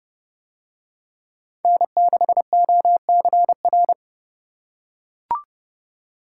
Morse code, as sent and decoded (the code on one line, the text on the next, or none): -. -.... --- -.-. .-.
N6OCR